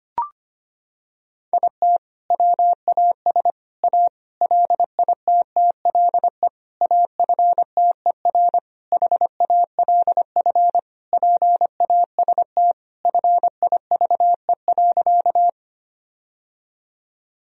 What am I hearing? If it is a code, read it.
Morse code, heard as a keyed tone, about 25 wpm, 700 Hz. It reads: IT WAS A LITTLE AFTER HALF PAST FIVE.
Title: Sentences from Top 1000 Words - Review Part 3 - 25wpm